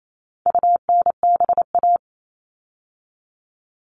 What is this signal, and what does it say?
Morse code, heard as a keyed tone, about 28 wpm, 700 Hz. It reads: UNBA
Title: All Cut Numbers - 4-Digit 28wpm